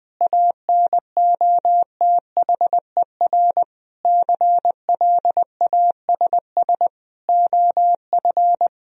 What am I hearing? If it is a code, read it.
Morse code, heard as a keyed tone, about 20 wpm, 700 Hz. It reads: ANOTHER CLASS OF
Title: Sentences from Top 1000 Words - Review Part 2 - 20wpm